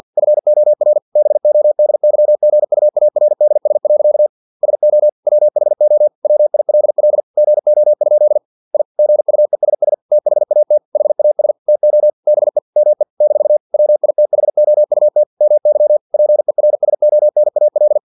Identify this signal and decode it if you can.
Morse code, heard as a keyed tone, about 50 wpm, 600 Hz. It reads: FOR BODYGUARDS. SO WHO WILL GO? I GUESS THAT HAS TO BE ME - WITHOUT MY PERSONAL
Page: Never 49wpm